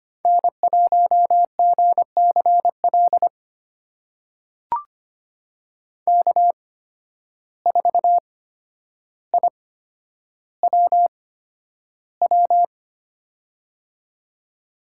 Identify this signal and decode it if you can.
Morse code, heard as a keyed tone, about 25 wpm, spaced as for 11 wpm, 700 Hz. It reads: N1GCL K4IWW
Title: USA Call Signs - 5 Characters - 8x Character Spacing - 25wpm